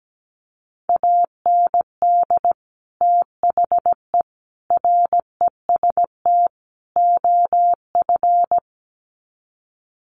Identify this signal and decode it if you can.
Morse code, heard as a keyed tone, about 17 wpm, 700 Hz. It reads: AND THE REST OF
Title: Sets of 4 Words 17wpm